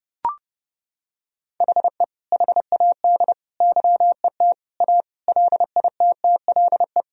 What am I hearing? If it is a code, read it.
Morse code, heard as a keyed tone, about 32 wpm, 700 Hz. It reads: HE HAD YET A LITTLE
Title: Sentences from Top 1000 Words - Review Part 2 - 30wpm